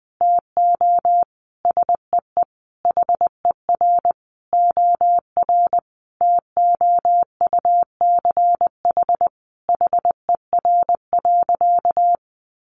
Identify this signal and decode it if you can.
Morse code, heard as a keyed tone, about 20 wpm, 700 Hz. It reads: TO SEE HER OR TOUCH HER.
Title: Sentences from Top 1200 Words 20wpm